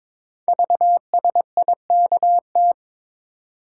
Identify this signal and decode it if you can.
Morse code, heard as a keyed tone, about 22 wpm, 700 Hz. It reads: VSIKT
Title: Code Groups - 5 Letters - 22wpm